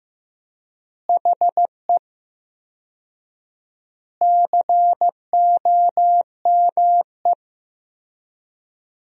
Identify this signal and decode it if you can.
Morse code, heard as a keyed tone, about 15 wpm, 700 Hz. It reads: HE COME
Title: 4x Word-Spacing - Sentences from Top 300 Words - 15wpm